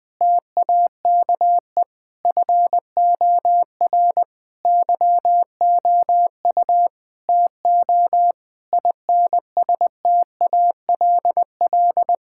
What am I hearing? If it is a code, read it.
Morse code, heard as a keyed tone, about 20 wpm, 700 Hz. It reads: TAKE FOR YOU TO INSTALL
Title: Sentences from Top 1600 Words 20wpm